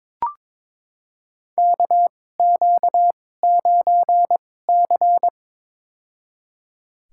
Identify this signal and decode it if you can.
Morse code, heard as a keyed tone, about 22 wpm, spaced as for 16 wpm, 700 Hz. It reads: KQ9C
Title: USA Call Signs - 4 Characters - 2x Character Spacing - 22wpm